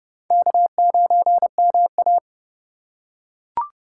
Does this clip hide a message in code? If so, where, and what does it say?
Yes: Morse code from 0.3 to 2.2 s, heard as a keyed tone, about 29 wpm, 700 Hz. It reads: K9MA